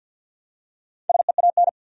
Morse code, heard as a keyed tone, about 53 wpm, 700 Hz. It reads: SEAN
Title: USA Common Names - 50wpm